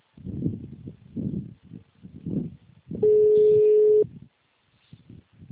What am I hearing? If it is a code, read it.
Ringback tone (Europe, 425 Hz), with speech.
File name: out- -2002- - - .29612